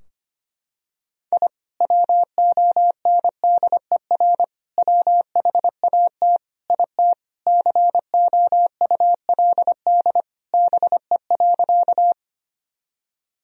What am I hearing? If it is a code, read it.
Morse code, heard as a keyed tone, about 25 wpm, 700 Hz. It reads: I WONDER WHAT IT COULD BE.
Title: Sentences from Top 1000 Words - Review Part 3 - 25wpm